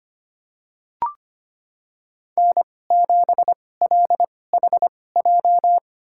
Morse code, heard as a keyed tone, about 25 wpm, spaced as for 18 wpm, 700 Hz. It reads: N7LHJ
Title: USA Call Signs - 5 Characters - 2x Character Spacing - 25wpm